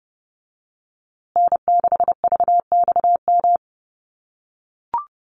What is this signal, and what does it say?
Morse code, heard as a keyed tone, about 30 wpm, 700 Hz. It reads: N6VXM
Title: USA Call Signs Repeated 3x 30wpm